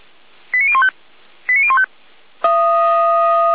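Five-tone selective call (ZVEI), heard as quick runs of five tones, tones of about 70 ms. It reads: 89015 89015
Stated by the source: ZVEI five-tone call received by SDR, from Feuerwehr Orlamünde recording server.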